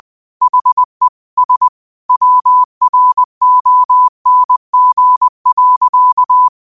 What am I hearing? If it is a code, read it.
Morse code, heard as a keyed tone, about 20 wpm, 1000 Hz. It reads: HE S WRONG.
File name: qotdfiles / 20wpm